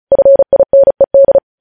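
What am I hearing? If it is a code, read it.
Morse code, heard as a keyed tone, about 36 wpm, 550 Hz. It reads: FINED
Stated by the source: News headlines sent in Morse code at 35 WPM.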